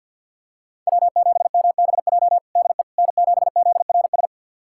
Morse code, heard as a keyed tone, about 47 wpm, 700 Hz. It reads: W7MBJ DE N7ZRS
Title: USA Callsign DE Callsign - 5 Characters - 50wpm